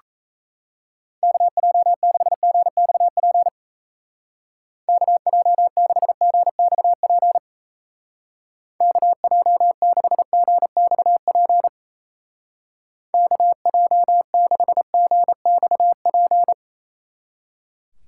Morse code, heard as a keyed tone, about 35 wpm, 700 Hz. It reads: KJ6GXP KJ6GXP KJ6GXP KJ6GXP